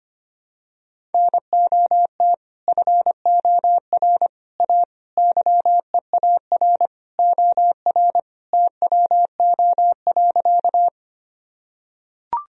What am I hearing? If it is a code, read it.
Morse code, heard as a keyed tone, about 25 wpm, 700 Hz. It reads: NOT FOR A YEAR OR TWO.
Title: Sentences from Top 1000 Words - Review Part 1 - 25wpm